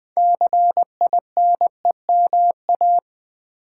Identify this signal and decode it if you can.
Morse code, heard as a keyed tone, about 20 wpm, 700 Hz. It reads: CINEMA
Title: Build A Word - 3 Letters At A Time - 6-Letter Words - 20wpm